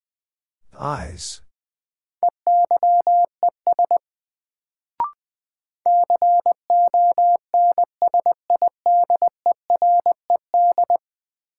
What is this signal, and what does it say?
Morse code, heard as a keyed tone, about 21 wpm, 700 Hz. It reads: EYES CONSIDERED
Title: Sets of 5 Words - New Word Drill - 20wpm